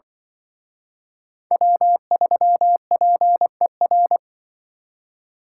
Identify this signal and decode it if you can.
Morse code, heard as a keyed tone, about 23 wpm, 700 Hz. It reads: W3PER
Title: Speed Racing - USA 1x3 Call Signs 20wpm